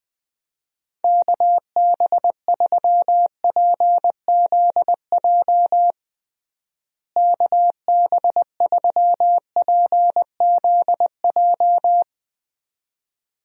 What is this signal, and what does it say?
Morse code, heard as a keyed tone, about 21 wpm, 700 Hz. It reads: KB3PZJ KB3PZJ
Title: USA Call Signs - 6 Characters - Repeated 2x - 20wpm